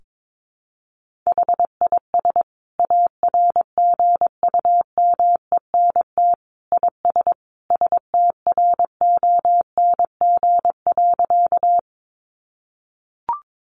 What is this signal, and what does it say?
Morse code, heard as a keyed tone, about 22 wpm, 700 Hz. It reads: HIS ARGUMENT IS STRONG.
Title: Sentences from Top 1400 Words 22wpm